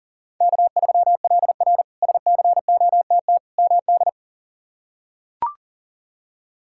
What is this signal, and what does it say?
Morse code, heard as a keyed tone, about 40 wpm, 700 Hz. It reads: K3LR SCOTT MD
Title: K1USN SST QSO Practice - 40wpm